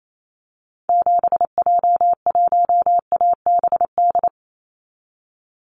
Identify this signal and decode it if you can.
Morse code, heard as a keyed tone, about 28 wpm, 700 Hz. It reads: 7J1ABD
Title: Call Signs Repeated 3x - 28wpm